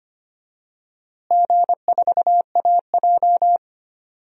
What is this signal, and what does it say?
Morse code, heard as a keyed tone, about 25 wpm, 700 Hz. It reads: G4AJ
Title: DXCC-100 Call Signs - 4 Characters - Repeated 2x - 25wpm